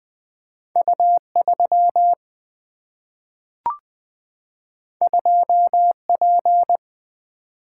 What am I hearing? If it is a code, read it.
Morse code, heard as a keyed tone, about 20 wpm, 700 Hz. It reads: U3 2P
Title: Single-Letter Single-Number 20wpm